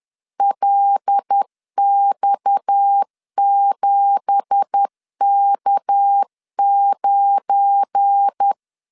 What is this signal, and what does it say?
Morse code, heard as a keyed tone, about 10 wpm, 800 Hz. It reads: LX7K9
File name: LEZIONI / 9 2867 10wpm 2733